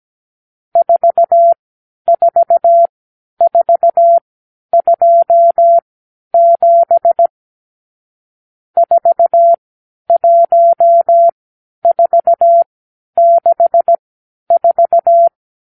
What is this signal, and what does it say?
Morse code, heard as a keyed tone, about 17 wpm, spaced as for 10 wpm, 680 Hz. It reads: 44427 41464
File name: k7qo / Lesson 52 (8)